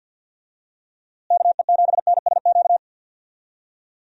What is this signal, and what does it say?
Morse code, heard as a keyed tone, about 50 wpm, 700 Hz. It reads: KE7NSQ
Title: USA Call Signs - 6 Characters - Repeated 3x - 50wpm